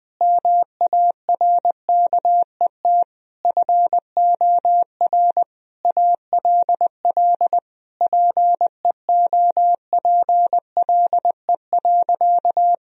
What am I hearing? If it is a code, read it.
Morse code, heard as a keyed tone, about 21 wpm, 700 Hz. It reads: MARKET FOR ALL PEOPLE.
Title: Sentences from Top 1000 Words - Review Part 3 - 20wpm